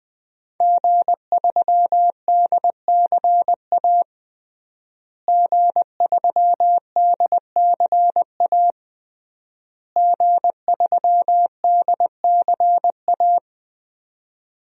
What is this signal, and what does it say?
Morse code, heard as a keyed tone, about 21 wpm, 700 Hz. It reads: G3DCA G3DCA G3DCA
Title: DXCC-100 Call Signs - 5 Characters - Repeated 3x - 20wpm